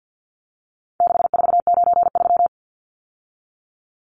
Morse code, heard as a keyed tone, about 50 wpm, 700 Hz. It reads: 6493
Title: Numbers - 4-Digit 50wpm